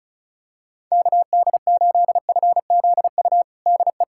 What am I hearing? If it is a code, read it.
Morse code, heard as a keyed tone, about 36 wpm, 700 Hz. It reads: KD8FZU DE
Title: USA Callsign DE Callsign 35wpm